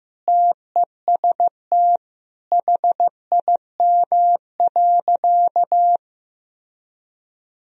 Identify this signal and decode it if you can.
Morse code, heard as a keyed tone, about 15 wpm, 700 Hz. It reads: TEST HIM.
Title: Sentences from Top 1000 Words - Review Part 3 - 15wpm